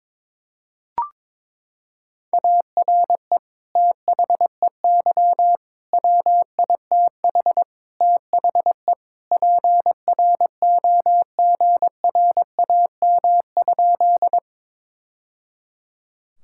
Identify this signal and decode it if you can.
Morse code, heard as a keyed tone, about 22 wpm, 700 Hz. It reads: ARE THEY WITH THE PROGRAM?
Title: Sentences from Top 1000 Words - Review Part 5 - 22wpm